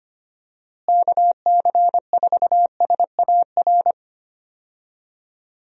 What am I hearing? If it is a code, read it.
Morse code, heard as a keyed tone, about 25 wpm, 700 Hz. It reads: KC4SAR
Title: USA Call Signs - 6 Characters - Repeated 2x - 25wpm